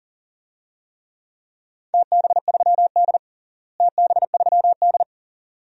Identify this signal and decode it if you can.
Morse code, heard as a keyed tone, about 40 wpm, 700 Hz. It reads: TB3D TB3D
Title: DXCC-100 Call Signs - 4 Characters - Repeated 2x - 40wpm